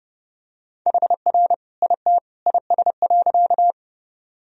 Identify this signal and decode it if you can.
Morse code, heard as a keyed tone, about 30 wpm, 700 Hz. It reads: HR IT IS.
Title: Sentences from Top 200 Words and Abbreviations 30wpm